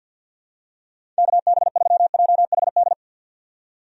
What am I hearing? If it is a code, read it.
Morse code, heard as a keyed tone, about 48 wpm, 700 Hz. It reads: KB3JHD
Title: USA Call Signs - 6 Characters - 50wpm